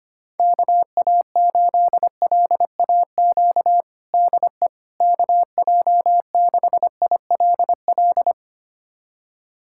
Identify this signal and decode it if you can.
Morse code, heard as a keyed tone, about 25 wpm, 700 Hz. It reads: KA8LAQ DE KJ6ILL
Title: USA Callsign DE Callsign 25wpm